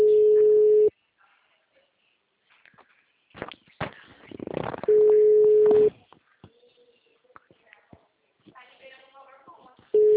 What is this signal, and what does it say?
Ringback tone (Europe, 425 Hz), with speech.